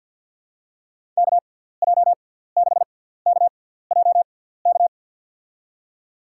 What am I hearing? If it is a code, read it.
Morse code, heard as a keyed tone, about 48 wpm, spaced as for 17 wpm, 700 Hz. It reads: KJ6KJK
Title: USA Call Signs - 6 Characters - 6x Character Spacing - 50wpm